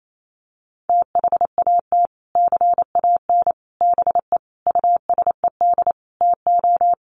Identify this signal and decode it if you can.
Morse code, heard as a keyed tone, about 28 wpm, 700 Hz. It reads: THAT CAN BE USED TO
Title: Sets of 5 Words Encore 28wpm